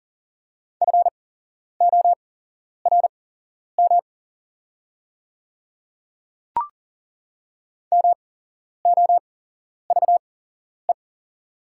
Morse code, heard as a keyed tone, about 37 wpm, spaced as for 11 wpm, 700 Hz. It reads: FORM MOVE
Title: 8x Character Spacing - Top 500 Words - 40wpm